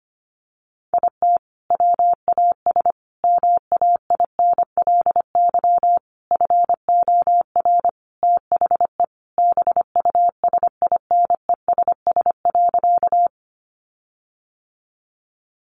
Morse code, heard as a keyed tone, about 25 wpm, 700 Hz. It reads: IT WAS MAINLY FOR THE BUSINESS.
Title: Sentences from Top 1400 Words 25wpm